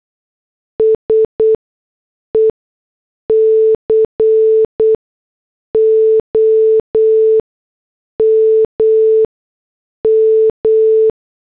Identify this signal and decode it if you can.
Morse code, heard as a keyed tone, about 8 wpm, 440 Hz. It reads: S E C O M M
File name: morse / headlines